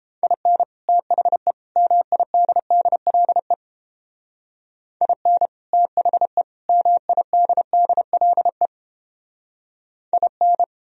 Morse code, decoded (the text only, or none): IN THE MIDDLE IN THE MIDDLE IN